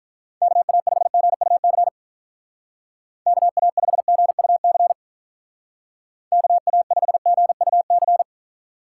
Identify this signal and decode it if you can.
Morse code, heard as a keyed tone, about 43 wpm, 700 Hz. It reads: KA5GUC KA5GUC KA5GUC
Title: Speed Racing - USA 2x3 Call Signs 35wpm